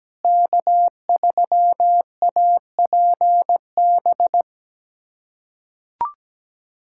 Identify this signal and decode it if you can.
Morse code, heard as a keyed tone, about 17 wpm, 700 Hz. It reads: K3APB
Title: USA Callsign DE Callsign - 5 Characters - 17wpm